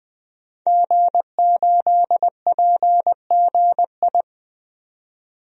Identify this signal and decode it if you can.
Morse code, heard as a keyed tone, about 20 wpm, 700 Hz. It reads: G8PGI
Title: DXCC-100 Call Signs - 5 Characters - 20wpm